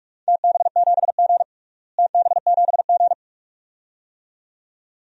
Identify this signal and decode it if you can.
Morse code, heard as a keyed tone, about 43 wpm, 700 Hz. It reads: TB7G TB7G
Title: DXCC-100 Call Signs - 4 Characters - Repeated 2x - 45wpm